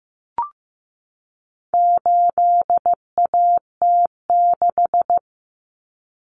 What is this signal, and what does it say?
Morse code, heard as a keyed tone, about 14 wpm, 700 Hz. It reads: 8AT6
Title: Common Cut Numbers - 4-Digit 15wpm